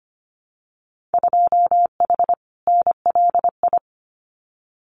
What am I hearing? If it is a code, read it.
Morse code, heard as a keyed tone, about 25 wpm, 700 Hz. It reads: 2H NLI
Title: Winter Field Day Exchanges 25wpm